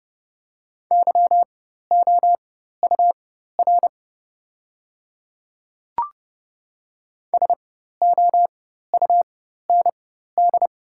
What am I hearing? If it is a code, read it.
Morse code, heard as a keyed tone, about 30 wpm, spaced as for 14 wpm, 700 Hz. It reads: YOUR SOUND
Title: 4x Character Spacing - Top 100 Words - 30wpm